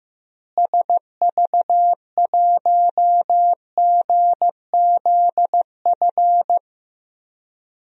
Morse code, heard as a keyed tone, about 15 wpm, 700 Hz. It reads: SV1GZF
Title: DXCC-100 Call Signs - 6 Characters - Repeated 3x - 15wpm